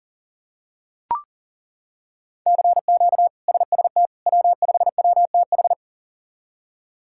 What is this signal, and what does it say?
Morse code, heard as a keyed tone, about 37 wpm, 700 Hz. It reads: CQ SST W5WTH
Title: K1USN SST QSO Practice - 40wpm